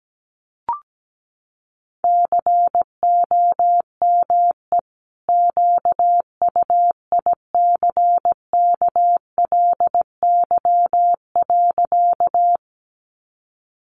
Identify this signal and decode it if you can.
Morse code, heard as a keyed tone, about 17 wpm, 700 Hz. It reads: COME QUICKLY.
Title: Sentences from Top 700 Words 17wpm